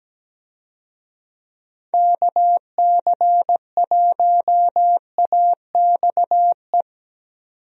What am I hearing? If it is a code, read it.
Morse code, heard as a keyed tone, about 17 wpm, 700 Hz. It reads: KC1AXE
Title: USA Call Signs - 6 Characters - 17wpm